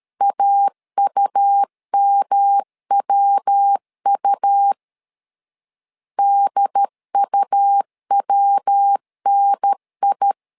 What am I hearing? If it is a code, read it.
Morse code, heard as a keyed tone, about 13 wpm, 800 Hz. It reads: AUMWU DUWNI
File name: LEZIONI / 3 MWRO / 3 MW 12wpm 1633